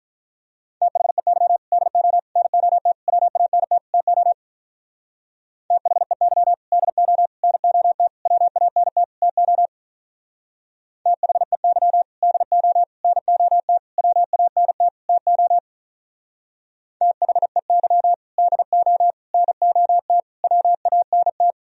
Morse code, heard as a keyed tone, about 46 wpm, 700 Hz. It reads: THEY DO NOT WANT TO THEY DO NOT WANT TO THEY DO NOT WANT TO THEY DO NOT WANT